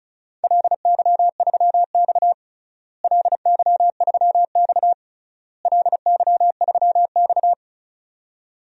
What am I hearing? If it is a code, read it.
Morse code, heard as a keyed tone, about 34 wpm, 700 Hz. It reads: LY3X LY3X LY3X
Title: DXCC-100 Call Signs - 4 Characters - Repeated 3x - 35wpm